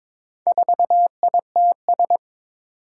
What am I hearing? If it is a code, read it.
Morse code, heard as a keyed tone, about 22 wpm, 700 Hz. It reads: 4ITS
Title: Four Challenging Letter-Number Combinations 22wpm